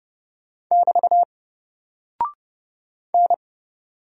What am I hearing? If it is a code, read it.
Morse code, heard as a keyed tone, about 30 wpm, 700 Hz. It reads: <BT> N
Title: Single Character (Letter:Number:Punctuation:Prosign) 30wpm